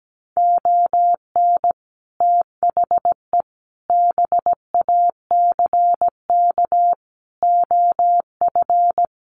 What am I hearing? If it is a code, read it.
Morse code, heard as a keyed tone, about 17 wpm, 700 Hz. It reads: ON THE BACK OF
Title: Sets of 4 Words 17wpm